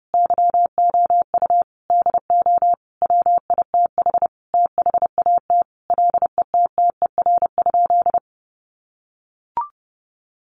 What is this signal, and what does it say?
Morse code, heard as a keyed tone, about 30 wpm, 700 Hz. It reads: YOU DO WITH THAT LETTER?
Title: Sentences from Top 300 Words 30wpm